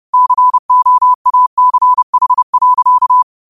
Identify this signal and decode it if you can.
Morse code, heard as a keyed tone, about 30 wpm, 1000 Hz. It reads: COACH.
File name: qotdfiles / 30wpm